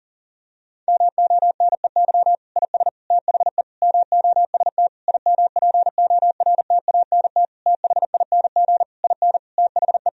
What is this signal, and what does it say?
Morse code, heard as a keyed tone, about 40 wpm, 700 Hz. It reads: MONEY IS THE MOST IMPORTANT THING IN THE
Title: Sentences from Top 1000 Words - Review Part 1 - 40wpm